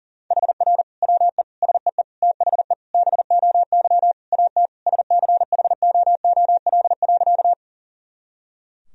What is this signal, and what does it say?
Morse code, heard as a keyed tone, about 40 wpm, 700 Hz. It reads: HR WE SEE THE BOY AT SCHOOL.
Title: Sentences from Top 300 Words and Abbreviations 40wpm